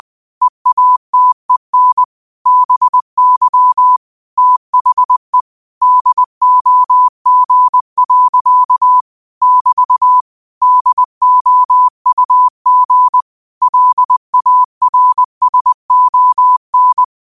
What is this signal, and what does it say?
Morse code, heard as a keyed tone, about 20 wpm, 1000 Hz. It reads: EATEN BY THE DOG. <BT> DOUG LARSON
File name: qotdfiles / 20wpm